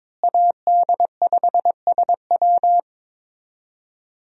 Morse code, heard as a keyed tone, about 22 wpm, 700 Hz. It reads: AD5SW
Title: USA Call Signs - 5 Characters - Repeated 3x - 22wpm